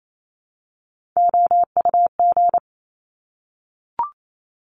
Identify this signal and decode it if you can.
Morse code, heard as a keyed tone, about 28 wpm, 700 Hz. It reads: OUG
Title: Top Three Letter Combinations in Words 28wpm